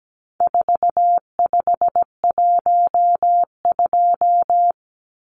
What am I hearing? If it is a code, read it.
Morse code, heard as a keyed tone, about 17 wpm, 700 Hz. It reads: 4512
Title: Numbers - 4-Digit 17wpm